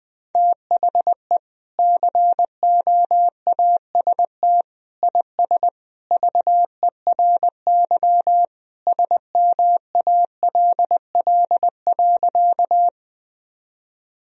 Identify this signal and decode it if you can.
Morse code, heard as a keyed tone, about 20 wpm, 700 Hz. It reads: THE COAST IS VERY SMALL.
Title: Sentences from Top 1000 Words - Review Part 4 - 20wpm